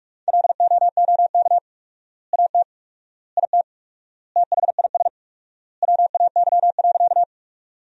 Morse code, heard as a keyed tone, about 46 wpm, 700 Hz. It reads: LOOK AT IT THIS WAY.
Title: 4x Word-Spacing - Sentences from Top 100 Words - 45wpm